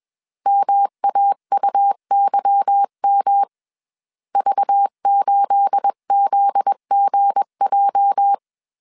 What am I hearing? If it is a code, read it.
Morse code, heard as a keyed tone, about 21 wpm, 800 Hz. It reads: MAUYM V8ZGJ
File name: CW / DET / 1 9 1500C S B 20wpm 2313